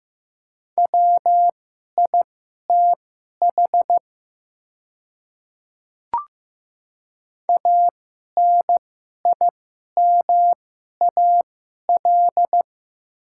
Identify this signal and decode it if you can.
Morse code, heard as a keyed tone, about 15 wpm, spaced as for 11 wpm, 700 Hz. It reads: WITH ANIMAL
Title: 2x Character Spacing - Top 200 Words - 15wpm